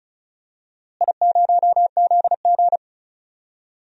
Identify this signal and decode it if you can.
Morse code, heard as a keyed tone, about 36 wpm, 700 Hz. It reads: I0ZG
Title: DXCC-100 Call Signs - 4 Characters - Repeated 3x - 35wpm